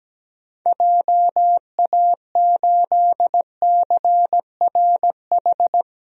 Morse code, heard as a keyed tone, about 17 wpm, 700 Hz. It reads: JA8CRH
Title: DXCC-100 Call Signs - 6 Characters - Repeated 3x - 17wpm